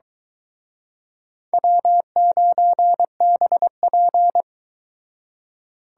Morse code, heard as a keyed tone, about 24 wpm, 700 Hz. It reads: W9BP